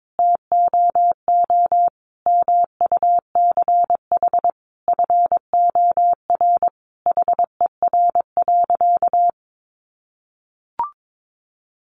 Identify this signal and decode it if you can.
Morse code, heard as a keyed tone, about 22 wpm, 700 Hz. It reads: TOO MUCH FOR HER.
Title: Sentences from Top 1200 Words 22wpm